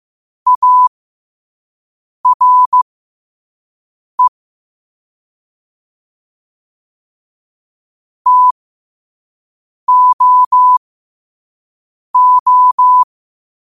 Morse code, heard as a keyed tone, about 15 wpm, spaced as for 5 wpm, 1000 Hz. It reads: ARE TOO
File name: qotdfiles / 5wpm